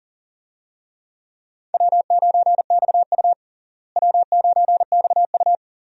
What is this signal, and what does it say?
Morse code, heard as a keyed tone, about 40 wpm, 700 Hz. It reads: W9XU W9XU
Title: USA Call Signs - 4 Characters - Repeated 2x - 40wpm